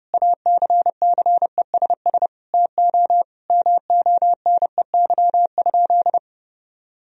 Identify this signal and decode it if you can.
Morse code, heard as a keyed tone, about 30 wpm, 700 Hz. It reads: ACCESS TO MONEY?
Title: Sentences from Top 1000 Words - Review Part 4 - 30wpm